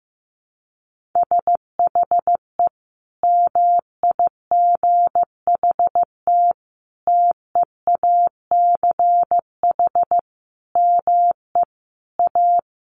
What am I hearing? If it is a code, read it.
Morse code, heard as a keyed tone, about 15 wpm, 700 Hz. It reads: SHE MIGHT TEACH ME A
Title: Sentences from Top 1200 Words 15wpm